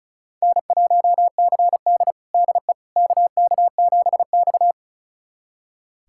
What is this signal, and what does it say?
Morse code, heard as a keyed tone, about 36 wpm, 700 Hz. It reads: N1CD DE KK7X
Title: USA Callsign DE Callsign - 4 Characters - 35wpm